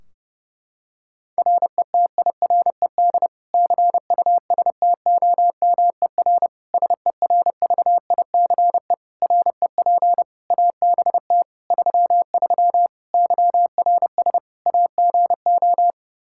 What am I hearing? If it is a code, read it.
Morse code, heard as a keyed tone, about 30 wpm, 700 Hz. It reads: RETIRED CUSTOMER SERVICE REP ABT 33 YRS AGO
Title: Practice QSO - Exchange 3 - More Info - 30wpm